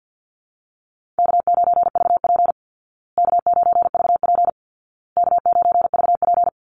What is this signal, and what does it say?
Morse code, heard as a keyed tone, about 50 wpm, 700 Hz. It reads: K9VP K9VP K9VP
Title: Call Signs Repeated 2x - 50wpm